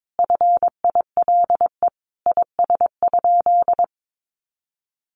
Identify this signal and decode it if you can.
Morse code, heard as a keyed tone, about 22 wpm, 700 Hz. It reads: FILE IS?
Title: Sentences from Top 1200 Words 22wpm